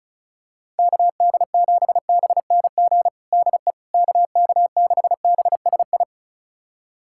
Morse code, heard as a keyed tone, about 34 wpm, 700 Hz. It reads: KD7BNG DE KK6BSI
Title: USA Callsign DE Callsign - 6 Characters - 35wpm